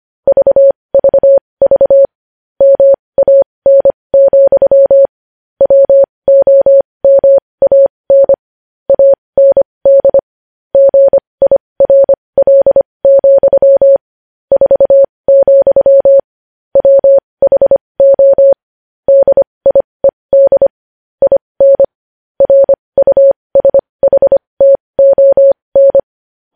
Morse code, heard as a keyed tone, about 25 wpm, spaced as for 20 wpm, 550 Hz. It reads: VVV MAN, WOMAN AND GIRL, 4, WHO DIED IN RUSHTON
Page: news headlines in morse code at 20 wpm